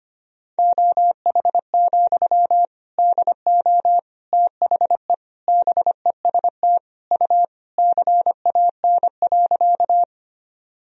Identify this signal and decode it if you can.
Morse code, heard as a keyed tone, about 25 wpm, 700 Hz. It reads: OH, DO THE BEST U CAN.
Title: Sentences from Top 500 Words and Abbreviations 25wpm